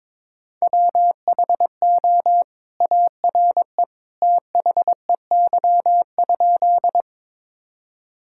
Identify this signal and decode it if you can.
Morse code, heard as a keyed tone, about 22 wpm, 700 Hz. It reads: WHO ARE THEY?
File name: practice / 22wpm / Sentences with 2-Word Phrases - 22wpm